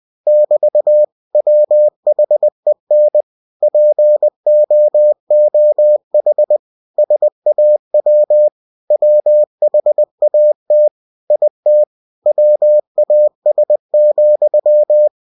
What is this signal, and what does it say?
Morse code, heard as a keyed tone, about 20 wpm, 600 Hz. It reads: <BT> WHEN POOH SAW WHAT IT WAS,
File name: cw-books / winnie-the-pooh / winnie-20wpm-page-0109